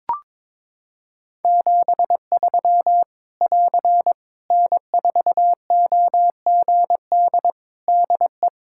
Morse code, heard as a keyed tone, about 21 wpm, 700 Hz. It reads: 73 <AR> N4OGD DE
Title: K1USN SST QSO Practice 22wpm